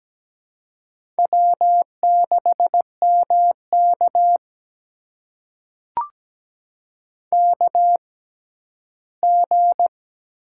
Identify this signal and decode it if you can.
Morse code, heard as a keyed tone, about 17 wpm, spaced as for 11 wpm, 700 Hz. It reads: W6MK KG
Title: USA Call Signs - 4 Characters - 6x Character Spacing - 17wpm